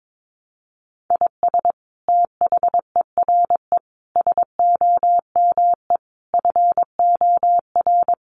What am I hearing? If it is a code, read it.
Morse code, heard as a keyed tone, about 22 wpm, 700 Hz. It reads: IS THERE SOME FOR
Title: Sentences from Top 200 Words 22wpm